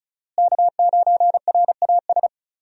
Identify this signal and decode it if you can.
Morse code, heard as a keyed tone, about 36 wpm, 700 Hz. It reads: K9RAS
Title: USA Call Signs - 5 Characters - Repeated 2x - 35wpm